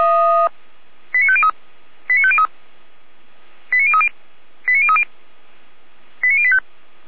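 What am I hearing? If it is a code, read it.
Five-tone selective call (ZVEI), heard as quick runs of five tones, tones of about 70 ms. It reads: 89503 89503 89030 89030 89085
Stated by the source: ZVEI five-tone selective call received on an SDR.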